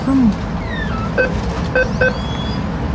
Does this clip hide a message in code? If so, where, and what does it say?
No.